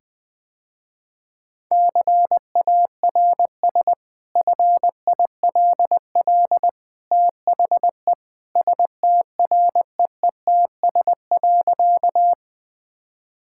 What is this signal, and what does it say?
Morse code, heard as a keyed tone, about 20 wpm, 700 Hz. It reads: CARS FILL THE STREETS.
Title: Easy Sentences with Hints - 20wpm